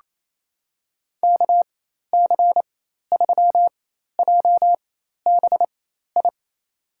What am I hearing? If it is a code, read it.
Morse code, heard as a keyed tone, about 28 wpm, spaced as for 13 wpm, 700 Hz. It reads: KC3JBI
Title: USA Call Signs - 4x Character Spacing - 28wpm